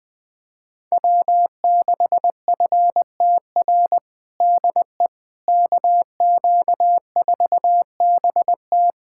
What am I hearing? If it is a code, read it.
Morse code, heard as a keyed tone, about 20 wpm, 700 Hz. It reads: W6FTR DE KQ4BT
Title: USA Callsign DE Callsign - 5 Characters - 20wpm